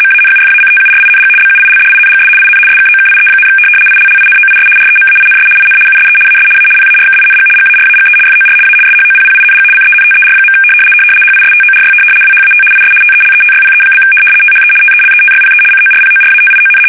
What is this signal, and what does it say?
RTTY, about 75 baud. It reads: MANUFACTURING BUILDINGS IN SKOKIE, ILLINOIS REMAINS IN USE AS A PARKING GARAGE FOR A SHOPPING CENTER. HISTORY WHAT IS TELETYPE? THE TELETYPE CORPORATION
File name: fsk850-75B / teletype-corp